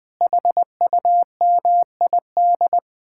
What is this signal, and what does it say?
Morse code, heard as a keyed tone, about 21 wpm, 700 Hz. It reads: HUMID
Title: Speed-Racing Common QSO Phrases Encore - 20wpm